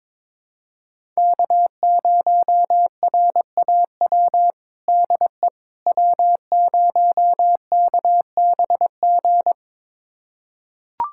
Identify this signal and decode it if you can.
Morse code, heard as a keyed tone, about 21 wpm, 700 Hz. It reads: K0RAW DE W0KBG
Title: USA Callsign DE Callsign - 5 Characters - 22wpm